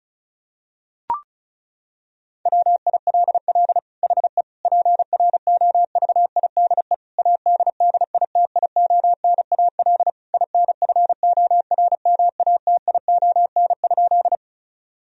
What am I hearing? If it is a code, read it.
Morse code, heard as a keyed tone, about 36 wpm, 700 Hz. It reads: WILL HE PROVIDE ADDITIONAL INFORMATION?
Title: Sentences from Top 1000 Words - Review Part 4 - 35wpm